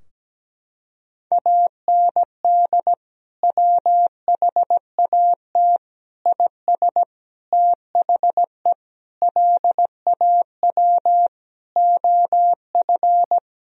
Morse code, heard as a keyed tone, about 17 wpm, 700 Hz. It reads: AND WHAT IS THE LAW OF
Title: Sentences from Top 1000 Words - Review Part 3 - 17wpm